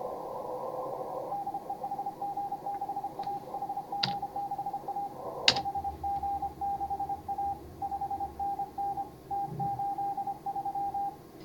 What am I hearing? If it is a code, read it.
Morse code, heard as a keyed tone, about 30 wpm, 800 Hz. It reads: DE HG6A HG6A HG6A 5NN T83
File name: CW / HA5MA RHOB HANG CW 2017